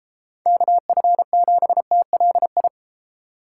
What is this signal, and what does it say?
Morse code, heard as a keyed tone, about 33 wpm, 700 Hz. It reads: KF7TLI